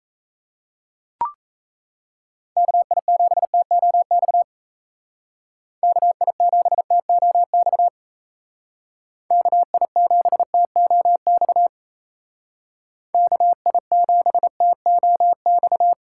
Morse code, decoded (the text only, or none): KI7TOX KI7TOX KI7TOX KI7TOX